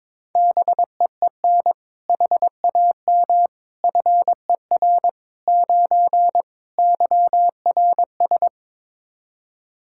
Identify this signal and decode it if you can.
Morse code, heard as a keyed tone, about 22 wpm, 700 Hz. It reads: BEEN HAM FER 9 YRS
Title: Practice QSO - Exchange 3 - More Info - 22wpm